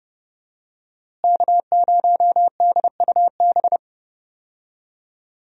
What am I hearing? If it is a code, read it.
Morse code, heard as a keyed tone, about 29 wpm, 700 Hz. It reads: K0DUB
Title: USA Call Signs - 5 Characters - Repeated 2x - 30wpm